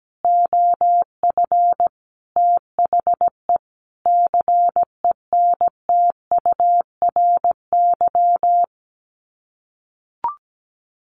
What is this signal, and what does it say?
Morse code, heard as a keyed tone, about 17 wpm, 700 Hz. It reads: OF THE CENTURY
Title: Sets of 5 Words 17wpm